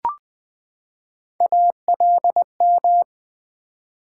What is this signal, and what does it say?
Morse code, heard as a keyed tone, about 20 wpm, 700 Hz. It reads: ALM
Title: Build A Word - 3 Letters At A Time - 6-Letter Words - 20wpm